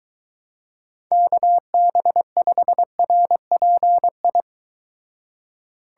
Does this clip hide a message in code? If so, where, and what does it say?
Yes: Morse code from 1.1 to 4.4 s, heard as a keyed tone, about 22 wpm, 700 Hz. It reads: KB5RPI